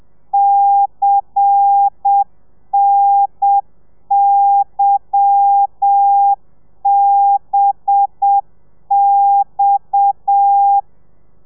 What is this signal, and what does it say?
Morse code, heard as a keyed tone, about 7 wpm, 800 Hz. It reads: CNYBX
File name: telegrafia / i2tfj / Lez10 30